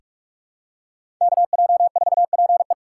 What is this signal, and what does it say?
Morse code, heard as a keyed tone, about 46 wpm, 700 Hz. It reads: K J 4 P E
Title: USA Call Signs - 5 Characters - 6x Character Spacing - 45wpm